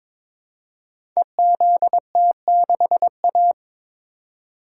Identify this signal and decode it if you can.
Morse code, heard as a keyed tone, about 22 wpm, 700 Hz. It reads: EZT6A
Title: Code Groups - 5 Letter:Numbers - 22wpm